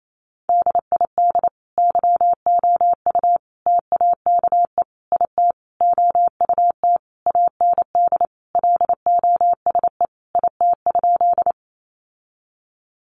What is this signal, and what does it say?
Morse code, heard as a keyed tone, about 28 wpm, 700 Hz. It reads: DID YOU TAKE IT OUT AND LOSE IT?
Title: Sentences from Top 1100 Words 28wpm